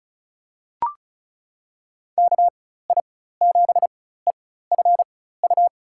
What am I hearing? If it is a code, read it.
Morse code, heard as a keyed tone, about 34 wpm, spaced as for 16 wpm, 700 Hz. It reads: KI7EFU
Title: USA Call Signs - 6 Characters - 4x Character Spacing - 35wpm